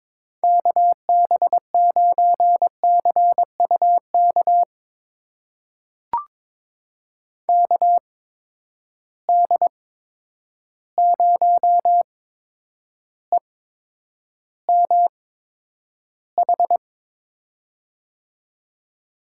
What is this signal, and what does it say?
Morse code, heard as a keyed tone, about 22 wpm, spaced as for 9 wpm, 700 Hz. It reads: KB9CUK KD0EMH
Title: USA Call Signs - 6 Characters - 8x Character Spacing - 22wpm